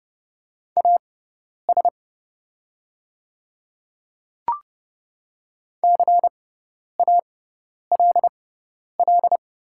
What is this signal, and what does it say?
Morse code, heard as a keyed tone, about 30 wpm, spaced as for 10 wpm, 690 Hz. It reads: AS CALL
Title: 6x Character Spacing - Top 200 Words - 30wpm